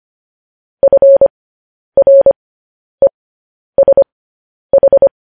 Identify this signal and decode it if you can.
Morse code, heard as a keyed tone, about 25 wpm, spaced as for 10 wpm, 550 Hz. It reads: FRESH